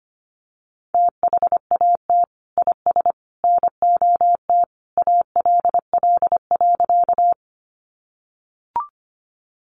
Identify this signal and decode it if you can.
Morse code, heard as a keyed tone, about 25 wpm, 700 Hz. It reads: THAT IS NOT ALL.
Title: Sentences from Top 600 Words 25wpm